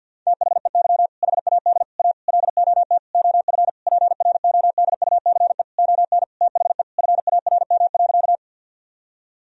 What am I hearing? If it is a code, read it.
Morse code, heard as a keyed tone, about 50 wpm, 700 Hz. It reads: THEY HAD A LOT OF PRODUCE ON THE FARM.
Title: Sentences from Top 1000 Words - Review Part 3 - 50wpm